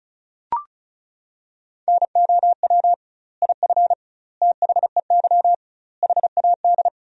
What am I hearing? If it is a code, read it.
Morse code, heard as a keyed tone, about 36 wpm, 700 Hz. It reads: NOW IF THEY HAD
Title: 2x Word-Spacing - Sentences from Top 500 Words - 35wpm